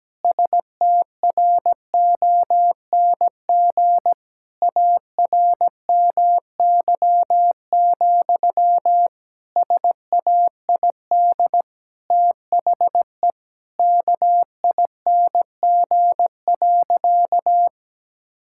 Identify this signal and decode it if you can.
Morse code, heard as a keyed tone, about 17 wpm, 700 Hz. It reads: STRONG ARMY, SAID THE KING.
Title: Sentences from Top 1000 Words - Review Part 5 - 17wpm